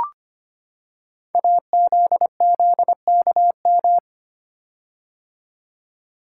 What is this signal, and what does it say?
Morse code, heard as a keyed tone, about 25 wpm, 700 Hz. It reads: AZZKM
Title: Code Groups - 5 Letters - 25wpm